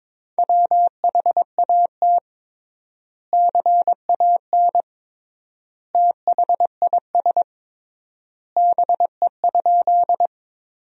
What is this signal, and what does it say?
Morse code, heard as a keyed tone, about 22 wpm, 700 Hz. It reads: WHAT CAN THIS BE?
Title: 3x Word-Spacing - Sentences from Top 100 Words - 22wpm